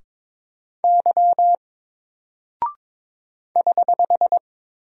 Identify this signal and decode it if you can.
Morse code, heard as a keyed tone, about 22 wpm, 700 Hz. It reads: Y <HH>
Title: Single Character (Letter:Number:Punctuation:Prosign) 22wpm